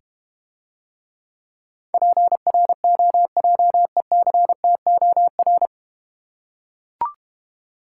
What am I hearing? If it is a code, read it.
Morse code, heard as a keyed tone, about 31 wpm, 700 Hz. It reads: PROJECTOR
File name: Easy Words - Speed-Racing with Hints - 25wpm